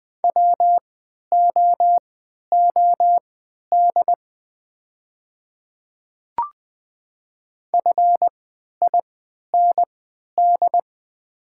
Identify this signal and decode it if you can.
Morse code, heard as a keyed tone, about 20 wpm, spaced as for 11 wpm, 700 Hz. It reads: WOOD FIND
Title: 3x Character Spacing - Top 300 Words - 20wpm